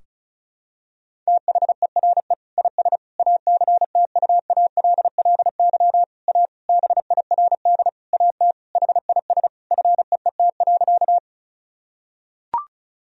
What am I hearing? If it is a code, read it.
Morse code, heard as a keyed tone, about 36 wpm, 700 Hz. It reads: THERE IS ACTUALLY A BIRD AT HIS FEET.
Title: Sentences from Top 1000 Words - Review Part 4 - 35wpm